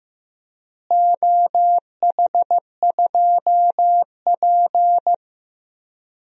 Morse code, heard as a keyed tone, about 15 wpm, 700 Hz. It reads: OH2P
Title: DXCC-100 Call Signs - 4 Characters - Repeated 3x - 15wpm